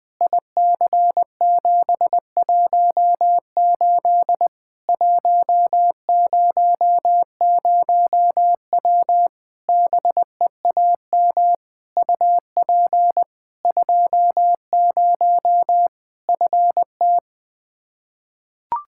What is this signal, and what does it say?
Morse code, heard as a keyed tone, about 20 wpm, 700 Hz. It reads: IC718 100W BEAM UP 20 FT
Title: K1USN SST QSO Practice 20wpm